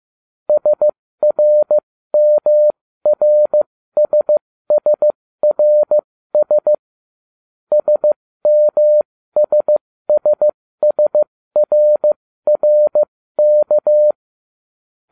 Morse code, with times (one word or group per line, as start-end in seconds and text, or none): 0.5-6.7 SRMRSSRS
7.7-14.1 SMSSSRRK